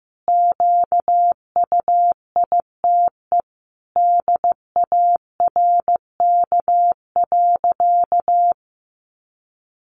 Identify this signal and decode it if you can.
Morse code, heard as a keyed tone, about 15 wpm, 700 Hz. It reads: QUITE DARK.
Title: Sentences from Top 700 Words 15wpm